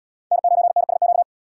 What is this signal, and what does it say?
Morse code, heard as a keyed tone, about 71 wpm, 700 Hz. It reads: N9NSQ